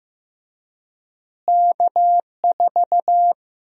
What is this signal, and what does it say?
Morse code, heard as a keyed tone, about 15 wpm, 700 Hz. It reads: K4
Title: USA Call Sign Prefixes - Repeated 2x - 15wpm